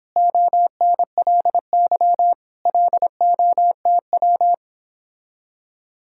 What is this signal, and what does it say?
Morse code, heard as a keyed tone, about 29 wpm, 700 Hz. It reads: ONLY LOTW